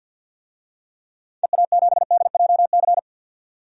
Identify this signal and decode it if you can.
Morse code, heard as a keyed tone, about 46 wpm, 700 Hz. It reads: EA7DJC
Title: DXCC-100 Call Signs - 6 Characters - 50wpm